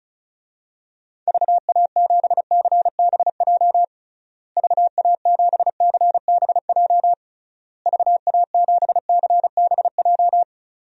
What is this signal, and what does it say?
Morse code, heard as a keyed tone, about 36 wpm, 700 Hz. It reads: VA7CBJ VA7CBJ VA7CBJ
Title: DXCC-100 Call Signs - 6 Characters - Repeated 3x - 35wpm